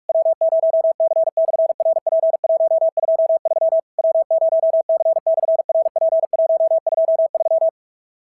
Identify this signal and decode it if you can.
Morse code, heard as a keyed tone, about 44 wpm, 660 Hz. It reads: W0C/RP123 W0C/RP123
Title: SOTA QSO Practice 45wpm